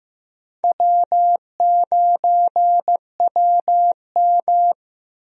Morse code, heard as a keyed tone, about 15 wpm, 700 Hz. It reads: W9WM
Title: USA Call Signs - 4 Characters - Repeated 3x - 15wpm